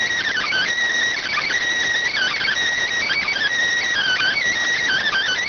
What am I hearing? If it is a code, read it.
Emergency Alert System (SAME) tones, heard as harsh buzzing bursts. It is heard from 0.3 to 4.4 s.